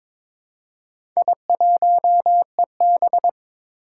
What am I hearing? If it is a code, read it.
Morse code, heard as a keyed tone, about 22 wpm, 700 Hz. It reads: I1EB
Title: Four Challenging Letter-Number Combinations 22wpm